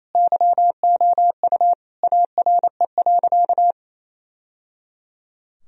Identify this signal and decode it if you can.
Morse code, heard as a keyed tone, about 28 wpm, 700 Hz. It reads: YOU ARE.
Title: Sentences from Top 1000 Words - Review Part 4 - 28wpm